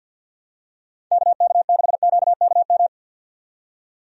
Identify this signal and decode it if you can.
Morse code, heard as a keyed tone, about 51 wpm, 700 Hz. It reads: KK6QKM
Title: USA Call Signs - 6 Characters - 50wpm